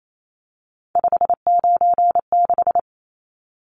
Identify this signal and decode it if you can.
Morse code, heard as a keyed tone, about 28 wpm, 700 Hz. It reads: 596
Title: Numbers - 3-Digit 28wpm